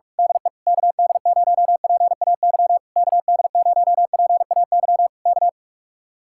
Morse code, heard as a keyed tone, about 45 wpm, 700 Hz. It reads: DE KD0PAY KD0PAY K
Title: Speed-Racing Common QSO Phrases Encore - 45wpm